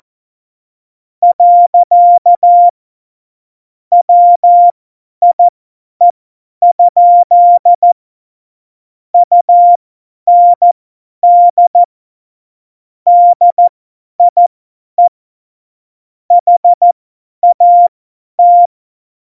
Morse code, heard as a keyed tone, about 14 wpm, spaced as for 10 wpm, 700 Hz. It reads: . WIE? UND DIE HAT
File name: Morsekurs / Witze / Witz 07 Juli 2020, 50 F70